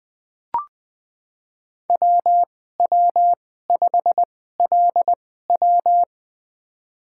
Morse code, heard as a keyed tone, about 20 wpm, spaced as for 15 wpm, 700 Hz. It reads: WW5LW
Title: USA Call Signs - 2x Character Spacing - 20wpm